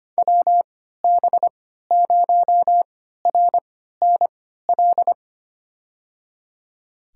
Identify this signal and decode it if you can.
Morse code, heard as a keyed tone, about 25 wpm, spaced as for 14 wpm, 700 Hz. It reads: WB0RNL
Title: USA Call Signs - 3x Character Spacing - 25wpm